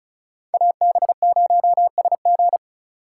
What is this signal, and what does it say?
Morse code, heard as a keyed tone, about 36 wpm, 700 Hz. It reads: AB0SG
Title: USA Call Signs - 2x Character Spacing - 35wpm